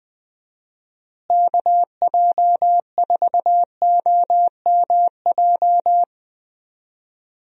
Morse code, heard as a keyed tone, about 20 wpm, 700 Hz. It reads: KJ4OMJ
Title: USA Call Signs - 6 Characters - Repeated 2x - 20wpm